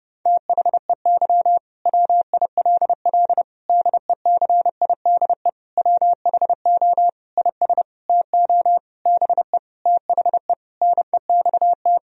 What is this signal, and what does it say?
Morse code, heard as a keyed tone, about 30 wpm, 700 Hz. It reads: THEY WILL DECIDE WHO IS TO BE THE NEXT
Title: Sentences from Top 1000 Words - Review Part 3 - 30wpm